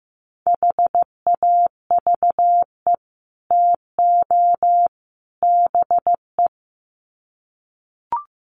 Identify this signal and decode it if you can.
Morse code, heard as a keyed tone, about 15 wpm, 700 Hz. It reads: HAVE TO BE
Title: Sets of 3 Words 15wpm